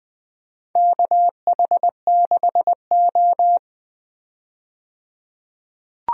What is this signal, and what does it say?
Morse code, heard as a keyed tone, about 20 wpm, 700 Hz. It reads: KH6O
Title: DXCC-100 Call Signs - 4 Characters - 20wpm